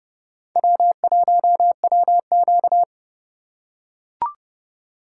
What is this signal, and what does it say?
Morse code, heard as a keyed tone, about 29 wpm, 700 Hz. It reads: W1WQ